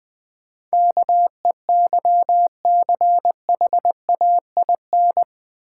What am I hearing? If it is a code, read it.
Morse code, heard as a keyed tone, about 20 wpm, 690 Hz. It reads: KEYCHAIN
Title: Difficult Words with Hints - 20wpm